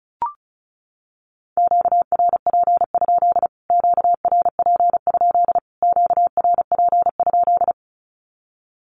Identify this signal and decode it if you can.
Morse code, heard as a keyed tone, about 34 wpm, 700 Hz. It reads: QRP? QRP? QRP?
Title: Q-Codes Repeated 3x 35wpm